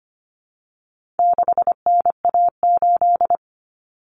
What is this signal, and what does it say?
Morse code, heard as a keyed tone, about 25 wpm, 700 Hz. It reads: 6NA8
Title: Common Cut Numbers - 4-Digit 25wpm